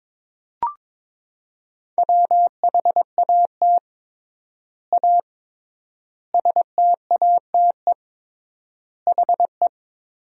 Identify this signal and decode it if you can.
Morse code, heard as a keyed tone, about 22 wpm, 700 Hz. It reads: WHAT A STATE HE
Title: 3x Word-Spacing - Sentences from Top 400 Words - 22wpm